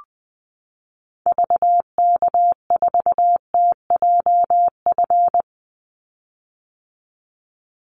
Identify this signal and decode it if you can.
Morse code, heard as a keyed tone, about 20 wpm, 700 Hz. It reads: VK4TJF
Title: Call Signs - 20wpm